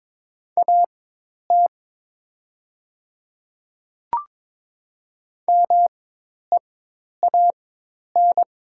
Morse code, heard as a keyed tone, about 22 wpm, spaced as for 10 wpm, 700 Hz. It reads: AT MEAN
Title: 4x Character Spacing - Top 500 Words - 22wpm